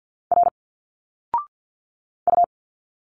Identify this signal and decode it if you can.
Morse code, heard as a keyed tone, about 47 wpm, 700 Hz. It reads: R U
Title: Single Letters 50wpm